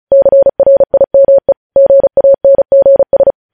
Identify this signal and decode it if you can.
Morse code, heard as a keyed tone, about 36 wpm, 550 Hz. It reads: CRIME GANGS
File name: podcast / News headlines-1x-35x35